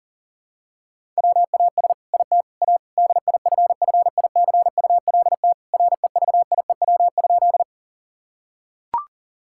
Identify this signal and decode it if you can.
Morse code, heard as a keyed tone, about 40 wpm, 700 Hz. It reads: WAS IT A DIFFICULT REVIEW?
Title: Sentences from Top 1000 Words - Review Part 6 - 40wpm